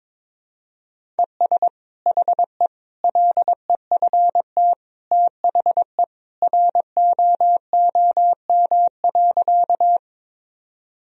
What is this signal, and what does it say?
Morse code, heard as a keyed tone, about 22 wpm, 700 Hz. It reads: ES HE LEFT THE ROOM.
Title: Sentences from Top 300 Words and Abbreviations 22wpm